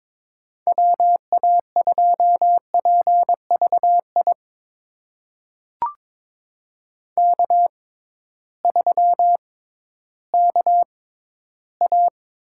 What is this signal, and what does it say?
Morse code, heard as a keyed tone, about 22 wpm, spaced as for 13 wpm, 700 Hz. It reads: WA2PVI K3KA
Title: USA Call Signs - 6x Character Spacing - 22wpm